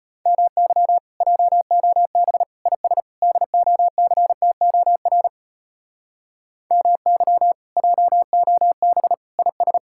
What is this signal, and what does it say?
Morse code, heard as a keyed tone, about 33 wpm, 700 Hz. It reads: MY JOB IS DOCTOR MY JOB IS